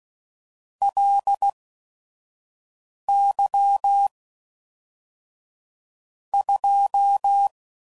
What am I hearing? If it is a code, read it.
Morse code, heard as a keyed tone, about 16 wpm, spaced as for 5 wpm, 800 Hz. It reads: LY 2
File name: CW Code Groups @5WPM / Technical Subjects / AND Gate